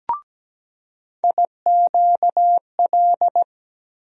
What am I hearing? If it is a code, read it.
Morse code, heard as a keyed tone, about 17 wpm, 700 Hz. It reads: IQL
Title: USA Call Sign Postfixes - 3 Letters 17wpm